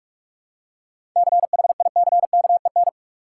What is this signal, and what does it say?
Morse code, heard as a keyed tone, about 45 wpm, 700 Hz. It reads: CHICKEN
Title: Seven Letter Words - 45wpm